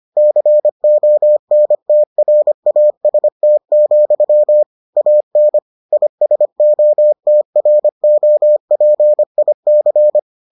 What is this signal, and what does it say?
Morse code, heard as a keyed tone, about 25 wpm, 600 Hz. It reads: CONTRAST, AN ISOTROPIC